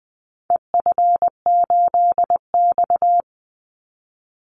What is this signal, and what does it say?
Morse code, heard as a keyed tone, about 20 wpm, 700 Hz. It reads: EF8X
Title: Call Signs - 20wpm